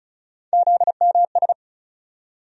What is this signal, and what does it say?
Morse code, heard as a keyed tone, about 34 wpm, 700 Hz. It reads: ZMS
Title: USA Call Sign Postfixes - 3 Letters 35wpm